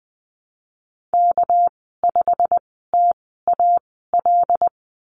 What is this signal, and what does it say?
Morse code, heard as a keyed tone, about 20 wpm, spaced as for 15 wpm, 700 Hz. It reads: K5TAL
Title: 231 Lesson 37g - Letter x - Call Signs 20wpm